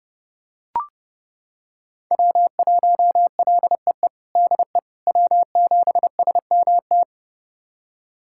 Morse code, heard as a keyed tone, about 30 wpm, 700 Hz. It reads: W1LEE DE W7SMT
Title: USA Callsign DE Callsign - 5 Characters - 30wpm